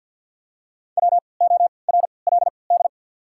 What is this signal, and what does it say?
Morse code, heard as a keyed tone, about 53 wpm, 700 Hz. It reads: W O R L D
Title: 3x Character Spacing - Top 200 Words - 50wpm